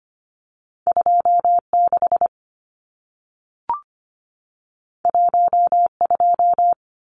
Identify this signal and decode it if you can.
Morse code, heard as a keyed tone, about 25 wpm, 700 Hz. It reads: 26 12
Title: Numbers - 2-Digit 25wpm